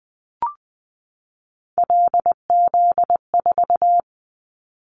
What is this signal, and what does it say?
Morse code, heard as a keyed tone, about 20 wpm, 700 Hz. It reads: LZ4
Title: Callsign Prefixes for Top 100 DXCC - 20wpm